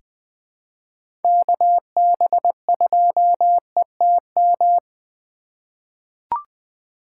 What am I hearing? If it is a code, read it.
Morse code, heard as a keyed tone, about 20 wpm, 700 Hz. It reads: KB2ETM
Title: USA Call Signs - 6 Characters - 8x Character Spacing - 20wpm